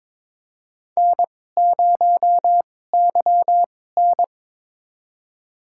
Morse code, heard as a keyed tone, about 22 wpm, spaced as for 16 wpm, 700 Hz. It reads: N0YN
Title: USA Call Signs - 4 Characters - 2x Character Spacing - 22wpm